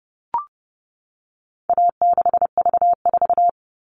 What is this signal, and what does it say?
Morse code, heard as a keyed tone, about 30 wpm, 700 Hz. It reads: A6V4
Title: All Cut Numbers - 4-Digit 30wpm